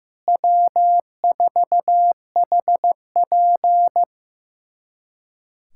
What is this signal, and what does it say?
Morse code, heard as a keyed tone, about 15 wpm, 700 Hz. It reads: W4HP
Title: USA Callsign DE Callsign - 4 Characters - 15wpm